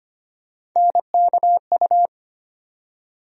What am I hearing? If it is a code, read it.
Morse code, heard as a keyed tone, about 25 wpm, 700 Hz. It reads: NKU
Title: USA Call Sign Postfixes - 3 Letters 25wpm